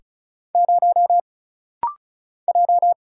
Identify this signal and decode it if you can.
Morse code, heard as a keyed tone, about 36 wpm, 700 Hz. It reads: 0 J
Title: Single Letter-Number Alt - Rapid-Fire 35wpm